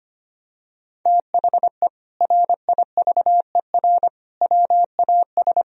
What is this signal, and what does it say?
Morse code, heard as a keyed tone, about 25 wpm, 700 Hz. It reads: THE RIVER WAS
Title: Sentences from Top 1000 Words - Review Part 1 - 25wpm